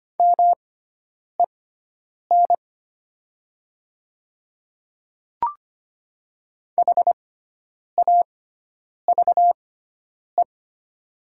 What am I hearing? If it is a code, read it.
Morse code, heard as a keyed tone, about 25 wpm, spaced as for 9 wpm, 700 Hz. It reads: MEN HAVE
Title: 6x Character Spacing - Top 200 Words - 25wpm